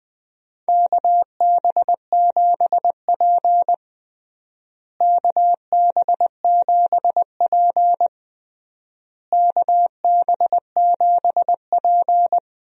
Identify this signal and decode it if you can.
Morse code, heard as a keyed tone, about 20 wpm, 700 Hz. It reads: KB7P KB7P KB7P
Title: USA Call Signs - 4 Characters - Repeated 3x - 20wpm